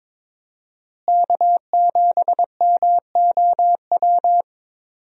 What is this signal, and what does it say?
Morse code, heard as a keyed tone, about 22 wpm, 700 Hz. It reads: K7MOW
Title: USA Call Signs - 5 Characters - Repeated 2x - 22wpm